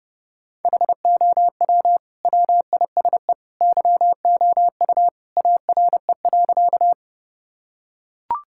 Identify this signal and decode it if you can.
Morse code, heard as a keyed tone, about 30 wpm, 700 Hz. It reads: HOW WISE YOU ARE.
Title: Sentences from Top 1600 Words 30wpm